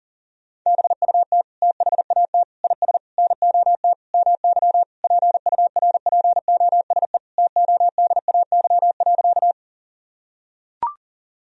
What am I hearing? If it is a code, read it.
Morse code, heard as a keyed tone, about 40 wpm, 700 Hz. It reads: BUT THAT IS NOT MY PURPOSE TODAY.
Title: Sentences from Top 1000 Words - Review Part 4 - 40wpm